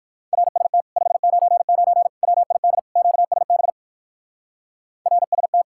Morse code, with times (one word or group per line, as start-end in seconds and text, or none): 0.3-0.8 RST
1.0-2.1 599
2.2-2.8 WID
3.0-3.7 QSB
5.1-5.6 RST